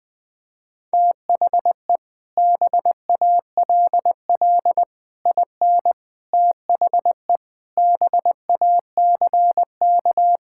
Morse code, heard as a keyed tone, about 20 wpm, 700 Hz. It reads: THE BALL IN THE BACK
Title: Sets of 8 Words - 20wpm